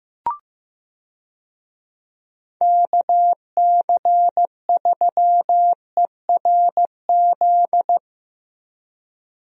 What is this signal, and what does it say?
Morse code, heard as a keyed tone, about 15 wpm, 700 Hz. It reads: KC3ERZ
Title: USA Call Signs - 6 Characters - Repeated 3x - 15wpm